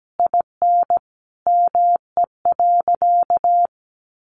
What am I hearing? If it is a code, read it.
Morse code, heard as a keyed tone, about 17 wpm, 700 Hz. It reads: IN ME.
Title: Sentences from Top 1500 Words 17wpm